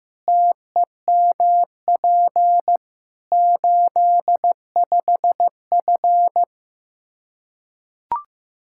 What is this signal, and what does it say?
Morse code, heard as a keyed tone, about 15 wpm, 700 Hz. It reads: TEMP 85F
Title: Practice QSO - Exchange 2 - Info - 15wpm